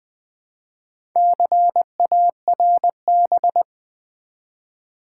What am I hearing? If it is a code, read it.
Morse code, heard as a keyed tone, about 20 wpm, 700 Hz. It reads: CARB
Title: Build A Word - 2 Letters At A Time - 6-Letter Words - 20wpm